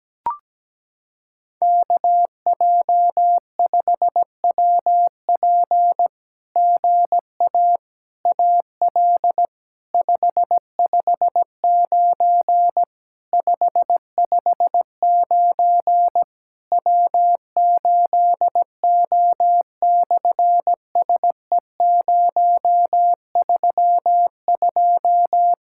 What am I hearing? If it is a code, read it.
Morse code, heard as a keyed tone, about 17 wpm, 700 Hz. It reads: KJ5WP GA AL 559 559 W8O/SE032
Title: SOTA QSO Practice 17wpm